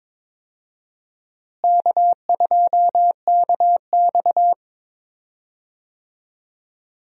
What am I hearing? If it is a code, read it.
Morse code, heard as a keyed tone, about 22 wpm, 700 Hz. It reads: K2KX
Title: USA Call Signs - 4 Characters - Repeated 3x - 22wpm